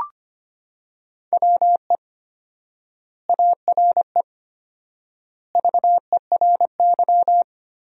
Morse code, heard as a keyed tone, about 25 wpm, 700 Hz. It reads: WE ARE VERY
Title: 4x Word-Spacing - Sentences from Top 300 Words - 25wpm